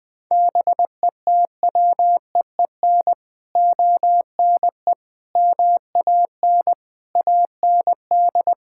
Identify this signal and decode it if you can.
Morse code, heard as a keyed tone, about 20 wpm, 700 Hz. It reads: BETWEEN ONE MAN AND
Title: Sets of 6 Words - 20wpm